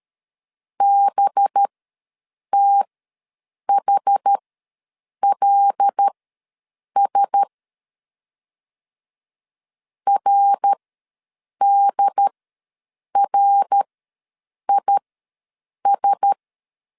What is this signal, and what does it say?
Morse code, heard as a keyed tone, about 13 wpm, spaced as for 7 wpm, 800 Hz. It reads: BTHLS RDRIS
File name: 5 LB ws 12wpm